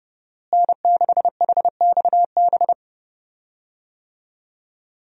Morse code, heard as a keyed tone, about 32 wpm, 700 Hz. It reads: N6HXB
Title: USA Call Signs - 5 Characters - Repeated 3x - 30wpm